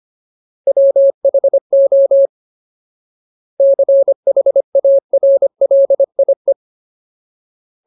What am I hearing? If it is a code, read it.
Morse code, heard as a keyed tone, about 25 wpm, 550 Hz. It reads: WHO CHARLIE
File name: Ultimate Word List / 25WPM Ultimate